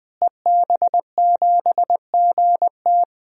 Morse code, heard as a keyed tone, about 21 wpm, 700 Hz. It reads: EB7GT
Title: DXCC-100 Call Signs - 5 Characters - Repeated 2x - 20wpm